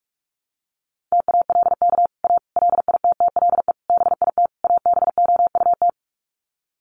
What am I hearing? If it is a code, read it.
Morse code, heard as a keyed tone, about 44 wpm, 700 Hz. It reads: TALK A LITTLE BIT ABOUT
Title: Sets of 5 Words Encore 45wpm